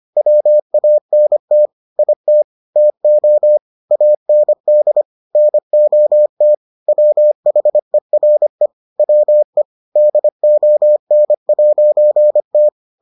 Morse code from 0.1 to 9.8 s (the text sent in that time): WANT IT TO AND NOT WHERE WE